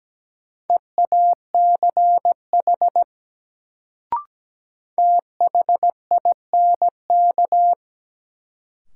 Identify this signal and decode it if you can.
Morse code, heard as a keyed tone, about 17 wpm, 700 Hz. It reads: EACH THINK
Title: Sentences from Top 200 Words - New Word Drill - 17wpm